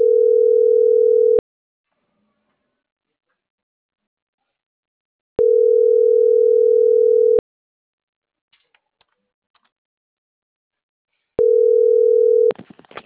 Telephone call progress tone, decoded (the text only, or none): ringback tone